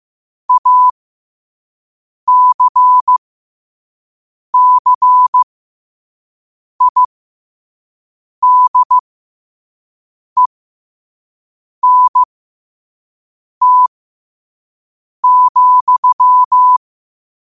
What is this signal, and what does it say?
Morse code, heard as a keyed tone, about 15 wpm, spaced as for 5 wpm, 1000 Hz. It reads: ACCIDENT,
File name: qotdfiles / 5wpm